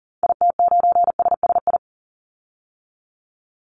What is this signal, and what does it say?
Morse code, heard as a keyed tone, about 40 wpm, 700 Hz. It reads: IT9SSI
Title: Call Signs Repeated 3x - 40wpm